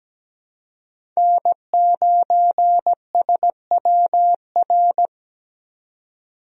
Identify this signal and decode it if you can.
Morse code, heard as a keyed tone, about 17 wpm, 700 Hz. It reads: N9SWR
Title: USA Call Signs - 5 Characters - 17wpm